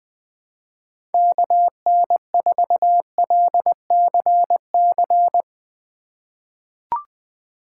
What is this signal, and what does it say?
Morse code, heard as a keyed tone, about 20 wpm, 700 Hz. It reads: KN4LCC
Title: USA Call Signs - 6 Characters - 4x Character Spacing - 20wpm